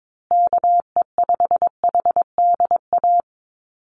Morse code, heard as a keyed tone, about 22 wpm, 700 Hz. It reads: KE5HDA
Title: USA Call Signs Repeated 2x 22wpm